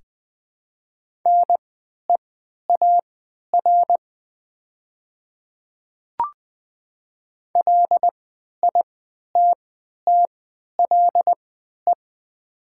Morse code, heard as a keyed tone, about 20 wpm, spaced as for 11 wpm, 700 Hz. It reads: NEAR LITTLE
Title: 3x Character Spacing - Top 300 Words - 20wpm